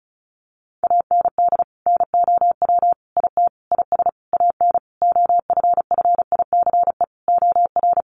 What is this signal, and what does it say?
Morse code, heard as a keyed tone, about 34 wpm, 700 Hz. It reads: AND NOW IT IS AN OFFICE OR
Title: Sentences from Top 1000 Words 35wpm